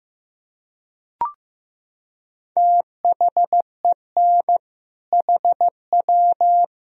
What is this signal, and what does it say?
Morse code, heard as a keyed tone, about 15 wpm, 700 Hz. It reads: THEN HW
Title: Sentences from Top 100 Words and Abbreviations 15wpm